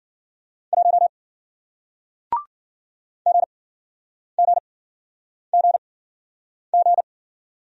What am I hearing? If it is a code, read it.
Morse code, heard as a keyed tone, about 53 wpm, spaced as for 9 wpm, 700 Hz. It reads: 1 GGGG